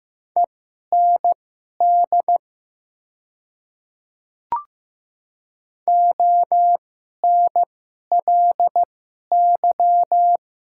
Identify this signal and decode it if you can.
Morse code, heard as a keyed tone, about 15 wpm, spaced as for 11 wpm, 700 Hz. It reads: END ONLY
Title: 2x Character Spacing - Top 300 Words - 15wpm